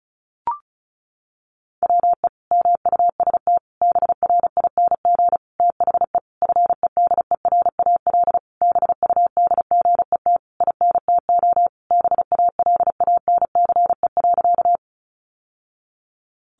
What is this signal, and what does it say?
Morse code, heard as a keyed tone, about 34 wpm, 700 Hz. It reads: WE MUST BRING THE FEDERAL BUDGET INTO BALANCE.
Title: Sentences from Top 1200 Words 35wpm